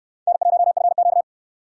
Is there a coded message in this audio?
Yes: Morse code from 0.3 to 1.2 s, heard as a keyed tone, about 68 wpm, 700 Hz. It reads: N1FQ